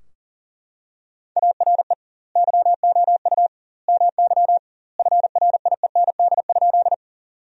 Morse code, heard as a keyed tone, about 40 wpm, 700 Hz. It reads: ARE YOU MY FRIEND?
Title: 2x Word-Spacing - Sentences from Top 300 Words - 40wpm